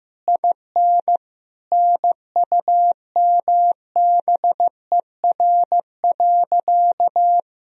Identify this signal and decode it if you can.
Morse code, heard as a keyed tone, about 15 wpm, 700 Hz. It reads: IN NUMBER.
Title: Sentences from Top 1000 Words - Review Part 1 - 15wpm